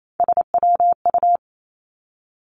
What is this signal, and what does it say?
Morse code, heard as a keyed tone, about 28 wpm, 700 Hz. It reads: SWU
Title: Three Challenging Letter-Number Combinations 28wpm